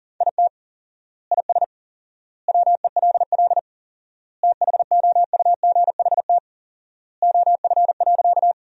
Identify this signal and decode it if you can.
Morse code, heard as a keyed tone, about 40 wpm, 700 Hz. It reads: IT IS WELL THOUGHT OF.
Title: 4x Word-Spacing - Sentences from Top 300 Words - 40wpm